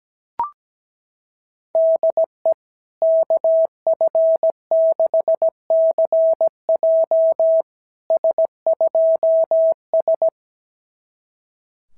Morse code, heard as a keyed tone, about 17 wpm, 650 Hz. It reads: DE KF6CJ S2S
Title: SOTA QSO Practice 17wpm